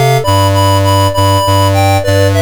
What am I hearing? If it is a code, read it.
There is no code.